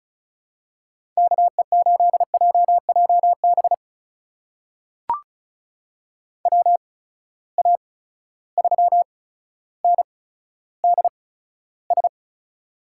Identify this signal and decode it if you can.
Morse code, heard as a keyed tone, about 36 wpm, spaced as for 15 wpm, 700 Hz. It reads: KE8JJB WA3NDS
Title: USA Call Signs - 6 Characters - 8x Character Spacing - 35wpm